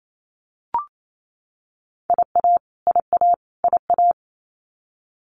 Morse code, heard as a keyed tone, about 28 wpm, 700 Hz. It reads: IA IA IA
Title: US States Repeated 3x 28wpm